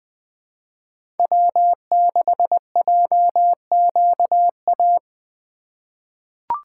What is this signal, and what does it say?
Morse code, heard as a keyed tone, about 20 wpm, 700 Hz. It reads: W6JQA
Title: USA Call Signs - 5 Characters - 3x Character Spacing - 20wpm